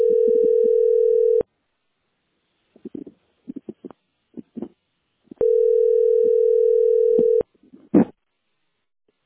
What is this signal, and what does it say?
Ringback tone (North America), 440 + 480 Hz.